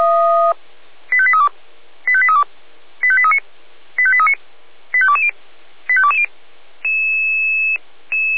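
Five-tone selective call (ZVEI), heard as quick runs of five tones, tones of about 70 ms. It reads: 85832 85832 85839 85839 85220 85220
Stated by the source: ZVEI five-tone call 85278 received by SDR.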